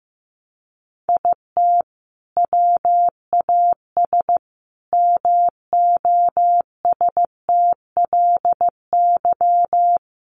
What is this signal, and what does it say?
Morse code, heard as a keyed tone, about 15 wpm, 700 Hz. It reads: IT WAS MOSTLY
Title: Sentences from Top 1300 Words 15wpm